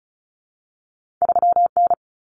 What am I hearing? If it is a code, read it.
Morse code, heard as a keyed tone, about 36 wpm, 700 Hz. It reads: 3N
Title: Common Cut Numbers - 2-Digit 35wpm